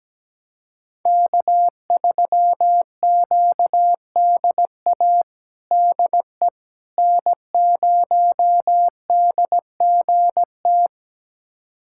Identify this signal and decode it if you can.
Morse code, heard as a keyed tone, about 17 wpm, 700 Hz. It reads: K3QDA DE N0DGT
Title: USA Callsign DE Callsign - 5 Characters - 17wpm